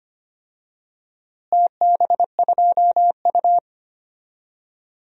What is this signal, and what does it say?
Morse code, heard as a keyed tone, about 25 wpm, 700 Hz. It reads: TB2U
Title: DXCC-100 Call Signs - 4 Characters - Repeated 3x - 25wpm